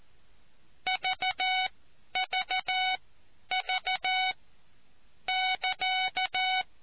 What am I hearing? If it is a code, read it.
Morse code, heard as a keyed tone, about 14 wpm, spaced as for 9 wpm, 750 Hz. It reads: VVV<KA>